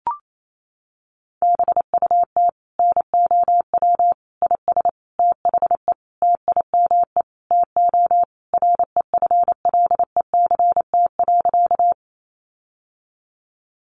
Morse code, heard as a keyed tone, about 28 wpm, 700 Hz. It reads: BUT NOW IS THE TIME TO REFLECT.
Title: Sentences from Top 1500 Words 28wpm